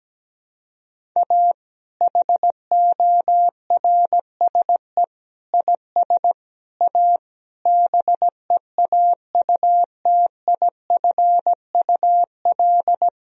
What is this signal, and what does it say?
Morse code, heard as a keyed tone, about 17 wpm, 700 Hz. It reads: A HORSE IS A BEAUTIFUL
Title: Sentences from Top 1000 Words - Review Part 1 - 17wpm